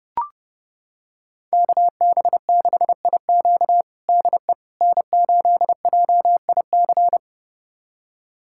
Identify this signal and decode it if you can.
Morse code, heard as a keyed tone, about 32 wpm, 700 Hz. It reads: KB6IQ DE N8JIC
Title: USA Callsign DE Callsign - 5 Characters - 30wpm